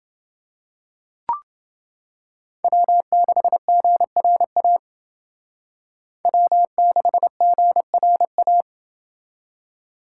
Morse code, heard as a keyed tone, about 28 wpm, 700 Hz. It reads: W6GRA W6GRA